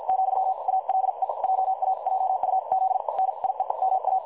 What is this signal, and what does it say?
Morse code, heard as a keyed tone, about 50 wpm, spaced as for 43 wpm, 760 Hz. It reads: , ALE JA JUZ NIE MA
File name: dlazaawansowanych / nagraniazpasma / websdr recording 3565.0kHz